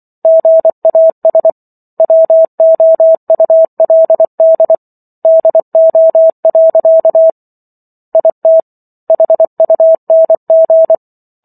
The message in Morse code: GAS WOULD DO. IT HUNG

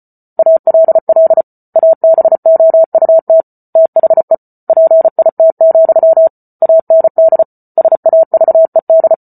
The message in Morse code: ALL ABOUT THE PIT, AND SAVED

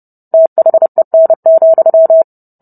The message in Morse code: THEN,

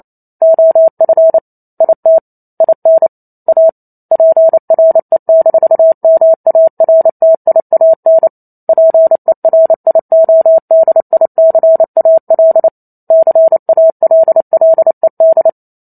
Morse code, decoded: OF IT IN A PRE-MARTIAN PERIODICAL CALLED

